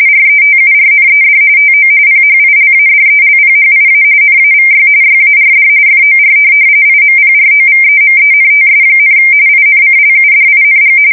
RTTY, about 45.45 baud. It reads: RETURNED TO PHILADELPHIA NAVY YARD FOR IMPROVEMENTS, INCLUDING
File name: fsk170 / USS-New-Jersey